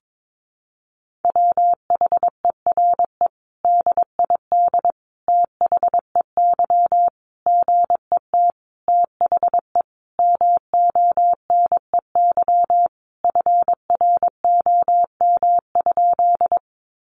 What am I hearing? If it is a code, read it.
Morse code, heard as a keyed tone, about 22 wpm, 700 Hz. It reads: WHERE DID THEY GET THE MONEY FROM?
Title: Sentences from Top 400 Words 22wpm